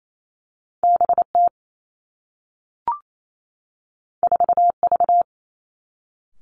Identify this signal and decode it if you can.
Morse code, heard as a keyed tone, about 28 wpm, 700 Hz. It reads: B T 4 V
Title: Two Challenging Letter-Number Combinations 28wpm